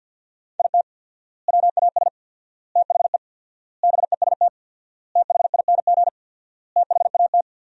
Morse code, heard as a keyed tone, about 50 wpm, 700 Hz. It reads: IT WAS THE BEST THING THAT
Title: 4x Word-Spacing - Sentences from Top 400 Words - 50wpm